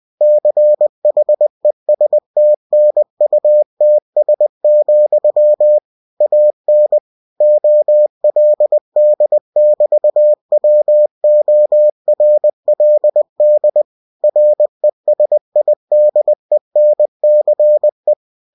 Morse code, heard as a keyed tone, about 20 wpm, 600 Hz. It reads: CHESTNUTS, AN OLD<BT>WORLD RESIDENCE
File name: cw-books / winnie-the-pooh / winnie-20wpm-page-0028